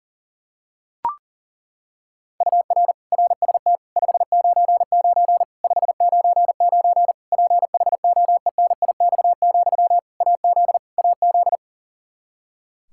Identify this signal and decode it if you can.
Morse code, heard as a keyed tone, about 40 wpm, 700 Hz. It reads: UR RST 599 599 PHOENIX, AZ AZ
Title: K1USN SST QSO Practice 40wpm